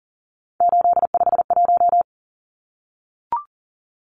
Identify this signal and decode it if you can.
Morse code, heard as a keyed tone, about 37 wpm, 700 Hz. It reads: 851
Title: Numbers - 3-Digit 40wpm